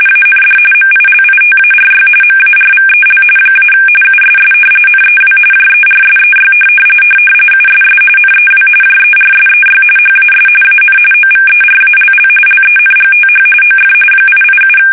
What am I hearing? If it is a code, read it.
RTTY, about 45.45 baud. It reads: AND MILITARY MARVEL - USS NEW JERSEY USS NEW JERSEY HAS AN OVERALL LENGTH OF 887
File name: fsk850 / battleshipnj